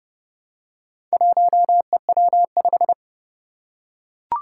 Morse code, heard as a keyed tone, about 30 wpm, 700 Hz. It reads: 1EW5
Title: Four Challenging Letter-Number Combinations 30wpm